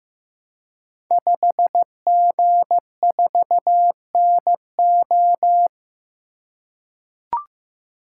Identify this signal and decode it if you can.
Morse code, heard as a keyed tone, about 15 wpm, 700 Hz. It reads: 5G4NO
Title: Code Groups - 5 Letter:Numbers - 15wpm